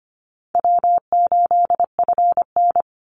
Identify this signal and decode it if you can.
Morse code, heard as a keyed tone, about 25 wpm, 700 Hz. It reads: W8FN
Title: Easy Call Signs Repeated 2x - 25wpm